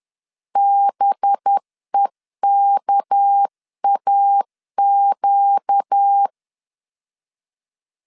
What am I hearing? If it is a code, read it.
Morse code, heard as a keyed tone, about 10 wpm, 800 Hz. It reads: BEKAQ
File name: CW / DET / 1 6 1500C S C 10wpm 4252